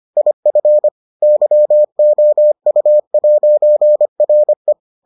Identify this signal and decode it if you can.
Morse code, heard as a keyed tone, about 25 wpm, 600 Hz. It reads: IF YOU'RE